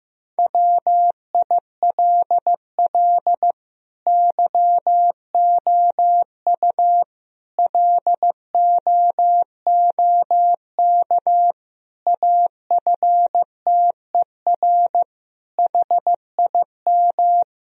Morse code, heard as a keyed tone, about 15 wpm, 700 Hz. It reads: WILL YOU LOOK AFTER HIM
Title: Sentences from Top 1000 Words - Review Part 1 - 15wpm